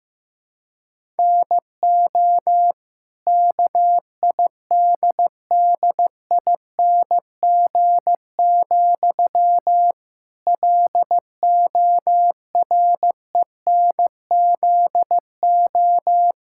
Morse code, heard as a keyed tone, about 15 wpm, 700 Hz. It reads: NO KIDDING, LORENZO
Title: Pangram Challenge v2 - 15wpm